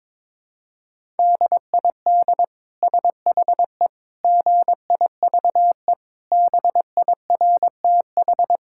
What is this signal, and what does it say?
Morse code, heard as a keyed tone, about 22 wpm, 700 Hz. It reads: DID SHE GIVE BIRTH
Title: Sentences from Top 1000 Words - Review Part 5 - 22wpm